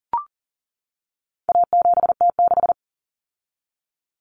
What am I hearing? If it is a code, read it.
Morse code, heard as a keyed tone, about 40 wpm, 700 Hz. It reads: A7T6
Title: Common Cut Numbers - 4-Digit 40wpm